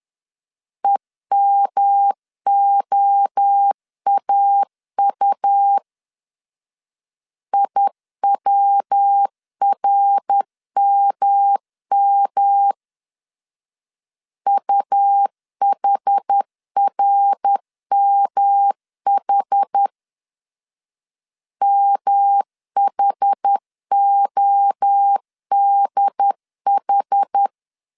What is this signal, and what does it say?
Morse code, heard as a keyed tone, about 10 wpm, 800 Hz. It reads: EMOAU IWRMM UHRMH MHODH
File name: CW / DET / 1 3 1500C S C 10wpm 3335